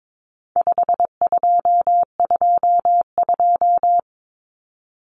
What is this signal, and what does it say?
Morse code, heard as a keyed tone, about 22 wpm, 700 Hz. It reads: 5222
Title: Numbers - 4-Digit 22wpm